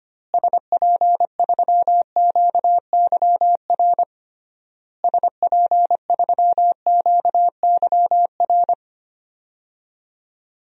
Morse code, heard as a keyed tone, about 25 wpm, 700 Hz. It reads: SP3QYR SP3QYR
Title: DXCC-100 Call Signs - 6 Characters - Repeated 3x - 25wpm